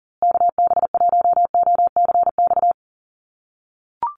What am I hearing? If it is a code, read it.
Morse code, heard as a keyed tone, about 40 wpm, 700 Hz. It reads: KB1OCX
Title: USA Call Signs Repeated 3x 40wpm